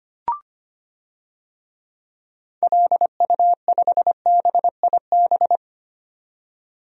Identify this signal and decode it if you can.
Morse code, heard as a keyed tone, about 25 wpm, 690 Hz. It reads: LU5BIB
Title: DXCC-100 Call Signs - 6 Characters - 25wpm